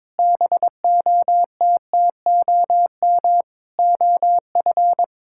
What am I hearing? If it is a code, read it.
Morse code, heard as a keyed tone, about 22 wpm, 700 Hz. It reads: BOTTOM OF
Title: Sentences from Top 1000 Words - Review Part 6 - 22wpm